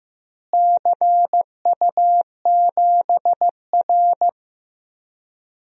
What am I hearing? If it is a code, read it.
Morse code, heard as a keyed tone, about 15 wpm, 700 Hz. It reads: CU7R
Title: DXCC-100 Call Signs - 4 Characters - Repeated 2x - 15wpm